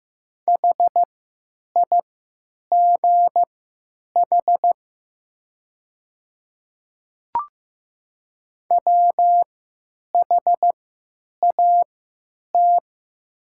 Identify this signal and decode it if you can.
Morse code, heard as a keyed tone, about 15 wpm, spaced as for 9 wpm, 700 Hz. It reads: HIGH WHAT
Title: 3x Character Spacing - Top 200 Words - 15wpm